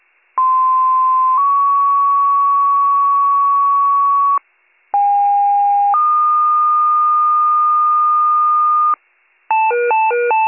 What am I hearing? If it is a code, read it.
Two-tone page, heard as a one-second tone, then a three-second tone, about 1040 then 1120 Hz, about 810 then 1220 Hz. It is heard from 0.4 to 8.9 s.